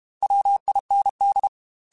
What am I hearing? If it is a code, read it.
Morse code, heard as a keyed tone, about 32 wpm, 800 Hz. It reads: WIND
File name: Weather Sets / Weather Sets-mp3s / Weather 5 22 @30